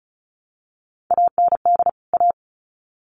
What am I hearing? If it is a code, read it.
Morse code, heard as a keyed tone, about 34 wpm, 700 Hz. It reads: AND A
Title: Sets of 2 Words 35wpm